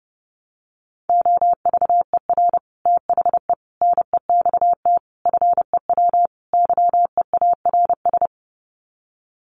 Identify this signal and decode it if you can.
Morse code, heard as a keyed tone, about 28 wpm, 700 Hz. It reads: OVER THE NEXT FEW YEARS
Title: Sets of 5 Words Encore 30wpm